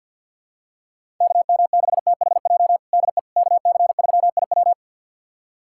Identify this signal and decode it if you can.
Morse code, heard as a keyed tone, about 50 wpm, 700 Hz. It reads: KM6THJ DE KC3IW
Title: USA Callsign DE Callsign 50wpm